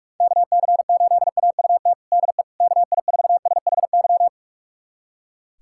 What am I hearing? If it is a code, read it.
Morse code, heard as a keyed tone, about 41 wpm, 700 Hz. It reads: KC8AUT DE KI4SHY
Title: USA Callsign DE Callsign - 6 Characters - 45wpm